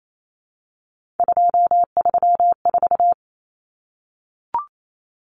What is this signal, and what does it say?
Morse code, heard as a keyed tone, about 28 wpm, 700 Hz. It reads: 234
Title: Numbers - 3-Digit 28wpm